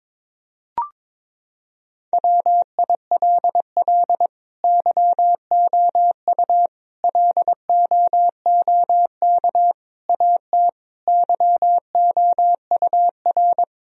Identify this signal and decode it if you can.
Morse code, heard as a keyed tone, about 23 wpm, 700 Hz. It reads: WILL YOU LOOK AT YOUR
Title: Sentences from Top 1000 Words - Review Part 1 - 22wpm